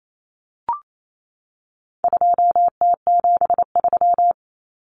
Morse code, heard as a keyed tone, about 28 wpm, 700 Hz. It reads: 2T73
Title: Common Cut Numbers - 4-Digit 28wpm